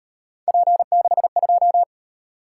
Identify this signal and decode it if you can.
Morse code, heard as a keyed tone, about 37 wpm, 700 Hz. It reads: P62